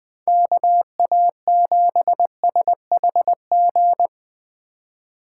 Morse code, heard as a keyed tone, about 19 wpm, 700 Hz. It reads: KA7SHG